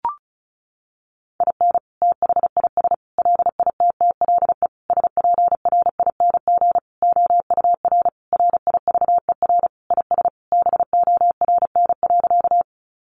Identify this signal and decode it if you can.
Morse code, heard as a keyed tone, about 34 wpm, 700 Hz. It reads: IN THIS LITTLE SPRING OUR RIVER IS BORN.
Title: Sentences from Top 1200 Words 35wpm